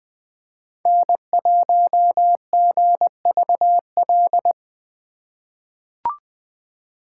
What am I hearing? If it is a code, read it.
Morse code, heard as a keyed tone, about 20 wpm, 700 Hz. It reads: N1GVL
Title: USA Call Signs - 3x Character Spacing - 20wpm